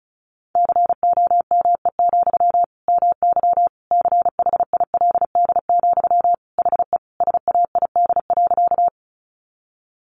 Morse code, heard as a keyed tone, about 34 wpm, 700 Hz. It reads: COME, MY CHILD, HE SAID.
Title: Sentences from Top 800 Words 35wpm